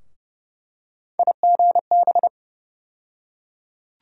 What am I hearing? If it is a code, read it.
Morse code, heard as a keyed tone, about 30 wpm, 700 Hz. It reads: IGB
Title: USA Call Sign Postfixes - 3 Letters 30wpm